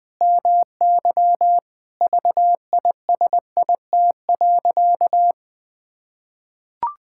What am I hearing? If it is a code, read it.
Morse code, heard as a keyed tone, about 19 wpm, 700 Hz. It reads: MY VISIT.
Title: Sentences from Top 1000 Words - Review Part 6 - 20wpm